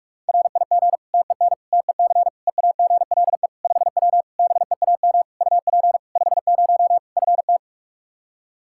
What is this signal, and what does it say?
Morse code, heard as a keyed tone, about 41 wpm, 700 Hz. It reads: RIG TEN TEC EAGLE 5W BEAM UP 50 FT
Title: Practice QSO - Exchange 2 - Info - 45wpm